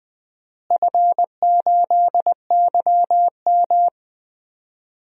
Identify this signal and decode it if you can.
Morse code, heard as a keyed tone, about 20 wpm, 700 Hz. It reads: F8YM
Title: DXCC-100 Call Signs - 4 Characters - Repeated 3x - 20wpm